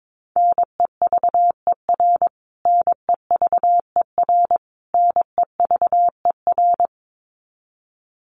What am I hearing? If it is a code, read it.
Morse code, heard as a keyed tone, about 22 wpm, 700 Hz. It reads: NEVER NEVER NEVER
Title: Top 500 Words Repeated 22wpm